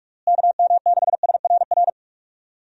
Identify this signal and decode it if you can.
Morse code, heard as a keyed tone, about 45 wpm, 700 Hz. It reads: KM6SRR
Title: USA Call Signs - 6 Characters - Repeated 3x - 45wpm